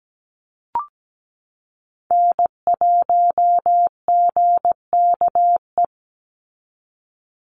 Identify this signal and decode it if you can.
Morse code, heard as a keyed tone, about 17 wpm, 700 Hz. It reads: N1GKE
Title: Call Signs Repeated 3x - 17wpm